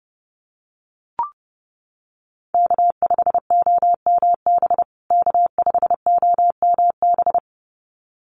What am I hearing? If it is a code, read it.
Morse code, heard as a keyed tone, about 30 wpm, 700 Hz. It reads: K5OMB K5OMB
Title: USA Call Signs Repeated 2x 30wpm